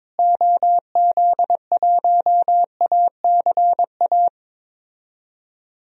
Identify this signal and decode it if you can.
Morse code, heard as a keyed tone, about 22 wpm, 700 Hz. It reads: OZ1ACA
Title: DXCC-100 Call Signs - 6 Characters - Repeated 3x - 22wpm